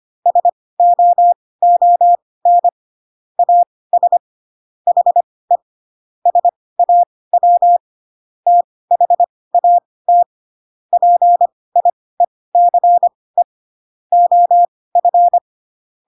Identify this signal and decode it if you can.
Morse code, heard as a keyed tone, about 25 wpm, spaced as for 18 wpm, 700 Hz. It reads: SOON AS HE SAW THAT PIECE OF
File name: Pinocchio / Pinocchio ch1 12 25 ik1hge